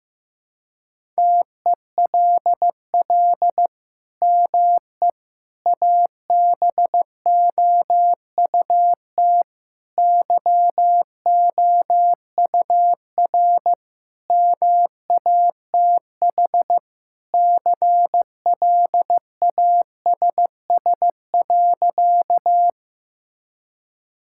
Morse code, heard as a keyed tone, about 15 wpm, 700 Hz. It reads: TELL ME ABOUT YOUR MATH CLASS.
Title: Sentences from Top 1000 Words - Review Part 6 - 15wpm